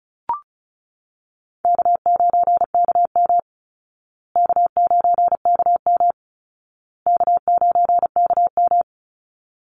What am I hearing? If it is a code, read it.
Morse code, heard as a keyed tone, about 34 wpm, 700 Hz. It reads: K9KM K9KM K9KM
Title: Call Signs Repeated 2x - 35wpm